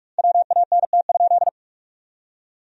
Morse code, heard as a keyed tone, about 46 wpm, 700 Hz. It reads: WANT?
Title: 2x Word-Spacing - Sentences from Top 200 Words - 45wpm